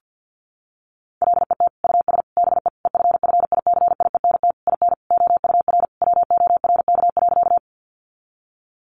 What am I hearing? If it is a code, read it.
Morse code, heard as a keyed tone, about 48 wpm, 700 Hz. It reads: LET US BE EFFICIENT IN OUR WORK.
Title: Sentences from Top 1300 Words 50wpm